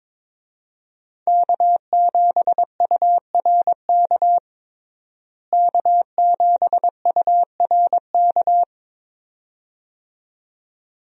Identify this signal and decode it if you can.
Morse code, heard as a keyed tone, about 22 wpm, 700 Hz. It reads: K7URK K7URK
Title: USA Call Signs - 5 Characters - Repeated 2x - 22wpm